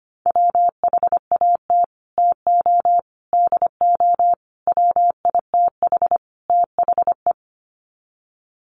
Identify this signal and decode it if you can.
Morse code, heard as a keyed tone, about 25 wpm, 700 Hz. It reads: WHAT TO DO WITH THE
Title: Sets of 5 Words Encore2 25wpm